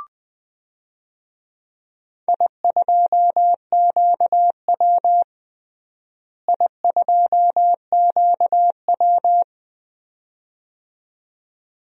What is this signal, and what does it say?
Morse code, heard as a keyed tone, about 20 wpm, 700 Hz. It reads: I2QW I2QW
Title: DXCC-100 Call Signs - 4 Characters - Repeated 2x - 20wpm